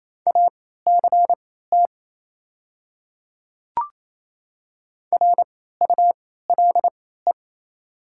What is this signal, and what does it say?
Morse code, heard as a keyed tone, about 27 wpm, spaced as for 16 wpm, 700 Hz. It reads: ACT RULE
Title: 3x Character Spacing - Top 500 Words - 28wpm